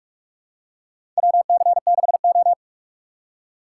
Morse code, heard as a keyed tone, about 46 wpm, 700 Hz. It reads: WC6O
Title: USA Call Signs - 3x Character Spacing - 45wpm